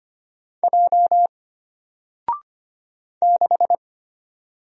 Morse code, heard as a keyed tone, about 25 wpm, 700 Hz. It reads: J 6
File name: Single-Letter-Number-25wpm